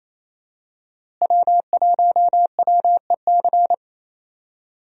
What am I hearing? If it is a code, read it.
Morse code, heard as a keyed tone, about 27 wpm, 700 Hz. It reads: W1WEC
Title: USA Call Signs - 5 Characters - Repeated 3x - 28wpm